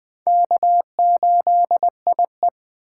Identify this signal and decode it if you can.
Morse code, heard as a keyed tone, about 20 wpm, 700 Hz. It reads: K8IE